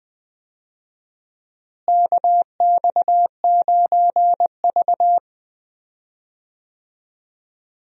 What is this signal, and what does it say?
Morse code, heard as a keyed tone, about 20 wpm, 700 Hz. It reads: KX9V
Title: USA Call Signs - 4 Characters - Repeated 2x - 20wpm